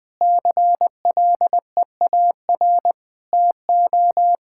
Morse code, heard as a keyed tone, about 19 wpm, 700 Hz. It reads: CLEAR TO
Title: Sentences with 9-Word Phrases - 20wpm